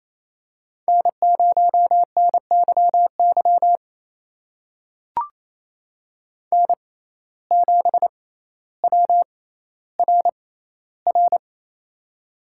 Morse code, heard as a keyed tone, about 28 wpm, spaced as for 14 wpm, 700 Hz. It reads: N0NYY N7WRR
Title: USA Call Signs - 5 Characters - 6x Character Spacing - 28wpm